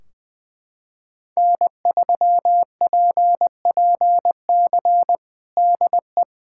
Morse code, heard as a keyed tone, about 20 wpm, 700 Hz. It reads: N3PPC DE
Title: K1USN SST QSO Practice 20wpm